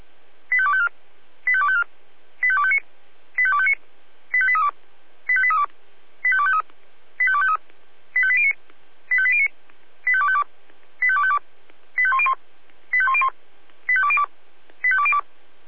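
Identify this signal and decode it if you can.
Five-tone selective call (ZVEI), heard as quick runs of five tones, tones of about 70 ms. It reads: 85254 85254 85259 85259 85832 85832 85253 85253 85809 85809 85252 85252 85101 85101 85202 85202
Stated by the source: ZVEI five-tone selective call received by SDR.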